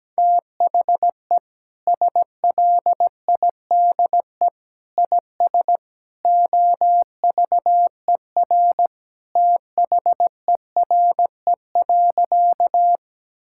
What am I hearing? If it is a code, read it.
Morse code, heard as a keyed tone, about 17 wpm, 700 Hz. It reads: THE SLIDE IS OVER THERE.
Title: Sentences from Top 1600 Words 17wpm